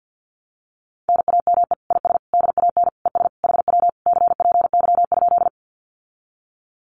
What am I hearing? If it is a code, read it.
Morse code, heard as a keyed tone, about 50 wpm, 700 Hz. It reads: NAME IS DAN ES HW CPY?
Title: Common QSO Phrases Encore 50wpm